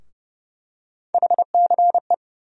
Morse code, heard as a keyed tone, about 30 wpm, 700 Hz. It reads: HCE
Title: USA Call Sign Postfixes - 3 Letters 30wpm